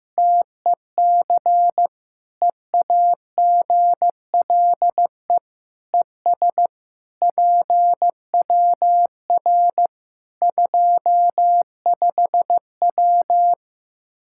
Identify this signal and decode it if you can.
Morse code, heard as a keyed tone, about 15 wpm, 700 Hz. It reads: TEC EAGLE ES PWR 25W
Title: K1USN SST QSO Practice 15wpm